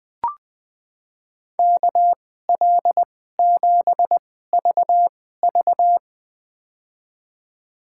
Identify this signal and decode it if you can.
Morse code, heard as a keyed tone, about 20 wpm, spaced as for 15 wpm, 700 Hz. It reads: KL7VV
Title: USA Call Signs - 5 Characters - 2x Character Spacing - 20wpm